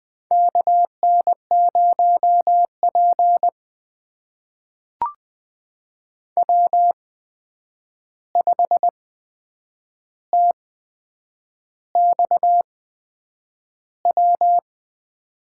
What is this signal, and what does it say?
Morse code, heard as a keyed tone, about 20 wpm, spaced as for 8 wpm, 700 Hz. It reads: KN0P W5TXW
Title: USA Call Signs - 8x Character Spacing - 20wpm